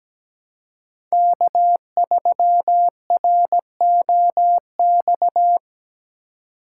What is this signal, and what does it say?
Morse code, heard as a keyed tone, about 17 wpm, 700 Hz. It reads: K3ROX
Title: USA Call Signs - 3x Character Spacing - 17wpm